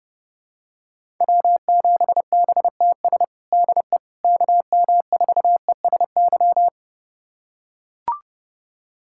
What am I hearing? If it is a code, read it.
Morse code, heard as a keyed tone, about 31 wpm, 700 Hz. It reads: W7BTS DE KM4ESY
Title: USA Callsign DE Callsign 30wpm